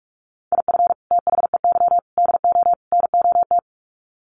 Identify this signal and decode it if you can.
Morse code, heard as a keyed tone, about 44 wpm, 700 Hz. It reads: IF THEY DO NOT
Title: Sets of 4 Words 45wpm